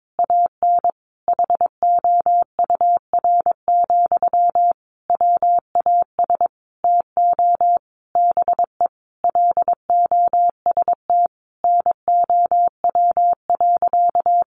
Morse code, heard as a keyed tone, about 22 wpm, 700 Hz. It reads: AN HOUR, WAS TO BE LOST NOW.
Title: Sentences from Top 1400 Words 22wpm